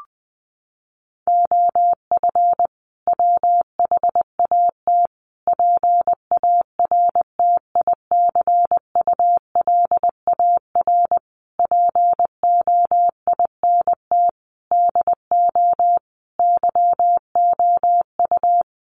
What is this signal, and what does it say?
Morse code, heard as a keyed tone, about 20 wpm, 700 Hz. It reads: OF WHAT PARTICULAR POINT DO YOU
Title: Sentences from Top 1300 Words 20wpm